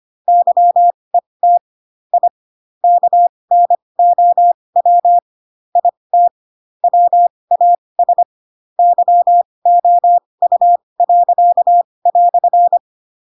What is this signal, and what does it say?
Morse code, heard as a keyed tone, about 25 wpm, spaced as for 20 wpm, 700 Hz. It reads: YET I KNOW IT WAS YOU."
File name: Pinocchio / Pinocchio ch2 15 25 ik1hge